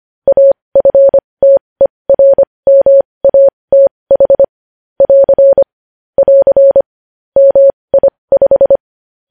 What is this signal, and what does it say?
Morse code, heard as a keyed tone, about 25 wpm, spaced as for 20 wpm, 550 Hz. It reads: AFTERMATH <AR> <AR> MI5
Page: news-headlines-in-morse-code-at-20-wpm